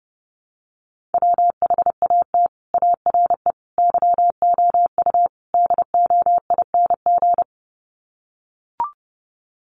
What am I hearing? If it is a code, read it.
Morse code, heard as a keyed tone, about 30 wpm, 700 Hz. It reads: WHAT ARE YOU DOING
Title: Sets of 4 Words 30wpm